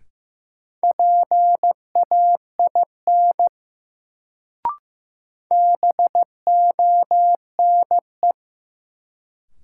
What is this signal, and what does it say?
Morse code, heard as a keyed tone, about 15 wpm, 700 Hz. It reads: PAIN BONE
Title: Sentences from Top 1600 Words - New Word Drill - 15wpm